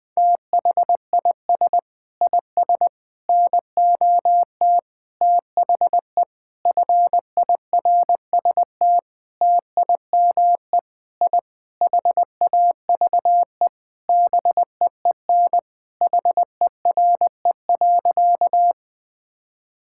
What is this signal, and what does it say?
Morse code, heard as a keyed tone, about 19 wpm, 700 Hz. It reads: THIS IS NOT THE FIRST TIME I HAVE BEEN HERE.
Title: Sentences with 6-Word Phrases - 20wpm